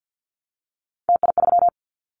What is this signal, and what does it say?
Morse code, heard as a keyed tone, about 50 wpm, 700 Hz. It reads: TI3
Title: Callsign Prefixes for Top 100 DXCC - 50wpm